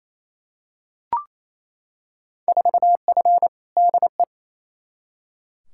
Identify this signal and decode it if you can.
Morse code, heard as a keyed tone, about 28 wpm, 700 Hz. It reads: 4F DE
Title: Field Day Exchanges Encore - 28wpm